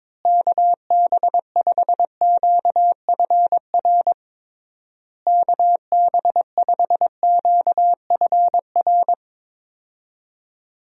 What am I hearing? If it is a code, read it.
Morse code, heard as a keyed tone, about 22 wpm, 700 Hz. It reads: KB5QFR KB5QFR
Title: USA Call Signs - 6 Characters - Repeated 3x - 22wpm